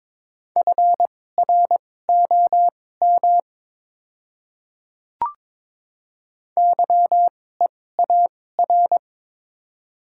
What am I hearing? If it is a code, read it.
Morse code, heard as a keyed tone, about 22 wpm, spaced as for 16 wpm, 700 Hz. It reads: FROM YEAR
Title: 2x Character Spacing - Top 200 Words - 22wpm